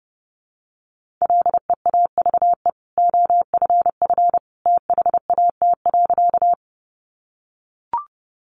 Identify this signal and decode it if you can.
Morse code, heard as a keyed tone, about 30 wpm, 700 Hz. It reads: LEAVE OFF THAT.
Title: Sentences from Top 400 Words 30wpm